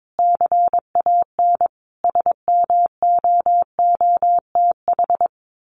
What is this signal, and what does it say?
Morse code, heard as a keyed tone, about 22 wpm, 700 Hz. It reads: CAN SMOOTH
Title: Sentences from Top 1400 Words 22wpm